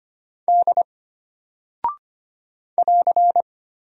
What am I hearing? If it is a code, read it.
Morse code, heard as a keyed tone, about 25 wpm, 700 Hz. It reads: D <AR>
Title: Single Character (Letter:Number:Punctuation:Prosign) Alt 25wpm